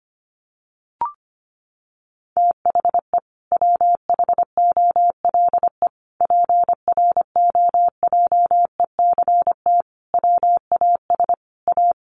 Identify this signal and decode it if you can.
Morse code, heard as a keyed tone, about 25 wpm, 700 Hz. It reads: THE WHOLE PROJECT WAS A
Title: Sentences from Top 1400 Words 25wpm